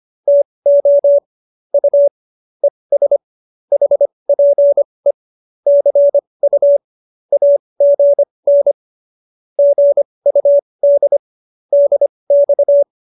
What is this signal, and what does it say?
Morse code, heard as a keyed tone, about 25 wpm, spaced as for 20 wpm, 570 Hz. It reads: TO U ES HPE CU AGN GUD DX